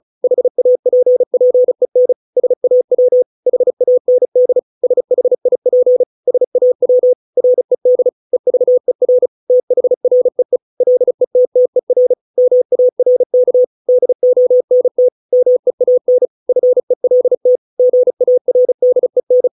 Morse code, heard as a keyed tone, about 34 wpm, 500 Hz. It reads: HAPPEN SAW HAND SHIP SAW RED EVER THREE LETTER MARK DONT MEAN FELT GARDEN